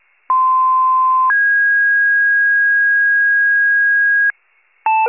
Two-tone page, heard as a one-second tone, then a three-second tone, about 1040 then 1650 Hz. It is heard from 0.3 to 4.3 s.